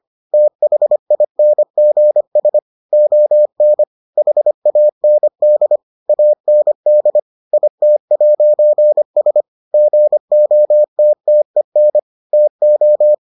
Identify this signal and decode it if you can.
Morse code, heard as a keyed tone, about 25 wpm, 600 Hz. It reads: THINGS ON HAND AND IT'S GOTTEN TO